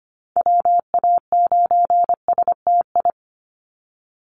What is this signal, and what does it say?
Morse code, heard as a keyed tone, about 25 wpm, 690 Hz. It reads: WA9STI
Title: NA SOTA Activator Callsigns 25wpm